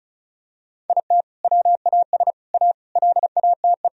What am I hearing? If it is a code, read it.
Morse code, heard as a keyed tone, about 34 wpm, 700 Hz. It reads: IT WAS A LATE
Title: Sentences from Top 1000 Words - Review Part 4 - 35wpm